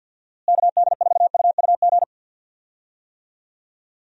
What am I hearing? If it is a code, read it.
Morse code, heard as a keyed tone, about 51 wpm, 700 Hz. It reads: KD4UUG
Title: USA Call Signs - 6 Characters - Repeated 2x - 50wpm